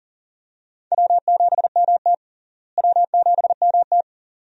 Morse code, heard as a keyed tone, about 39 wpm, 700 Hz. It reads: W7MT W7MT
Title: USA Call Signs - 4 Characters - Repeated 2x - 40wpm